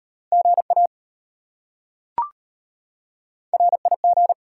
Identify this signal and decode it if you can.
Morse code, heard as a keyed tone, about 33 wpm, 690 Hz. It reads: GA RIG